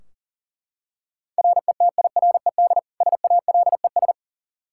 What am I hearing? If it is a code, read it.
Morse code, heard as a keyed tone, about 40 wpm, 690 Hz. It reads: RETIRED SALES
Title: Practice QSO - Exchange 3 - More Info - 40wpm